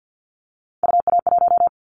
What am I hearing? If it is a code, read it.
Morse code, heard as a keyed tone, about 50 wpm, 700 Hz. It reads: UA1
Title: Callsign Prefixes for Top 100 DXCC - 50wpm